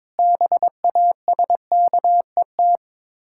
Morse code, heard as a keyed tone, about 22 wpm, 690 Hz. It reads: BASKET
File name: Difficult Words with Hints - 22wpm